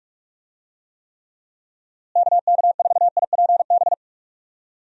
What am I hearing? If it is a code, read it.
Morse code, heard as a keyed tone, about 46 wpm, 700 Hz. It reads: KK4IPB
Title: USA Call Signs - 6 Characters - Repeated 2x - 45wpm